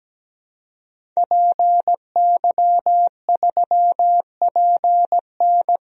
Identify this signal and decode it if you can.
Morse code, heard as a keyed tone, about 17 wpm, 700 Hz. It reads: PY3PN
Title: DXCC-100 Call Signs - 5 Characters - Repeated 2x - 17wpm